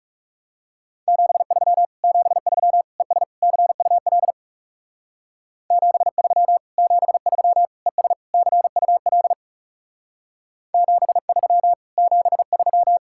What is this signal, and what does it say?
Morse code, heard as a keyed tone, about 36 wpm, 700 Hz. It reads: 73 73 ES CUL 73 73 ES CUL 73 73
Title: Speed-Racing Common QSO Phrases - 30wpm